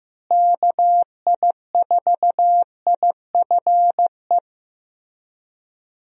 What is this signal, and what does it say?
Morse code, heard as a keyed tone, about 16 wpm, 700 Hz. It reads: KI4IFE
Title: USA Call Signs - 6 Characters - Repeated 3x - 15wpm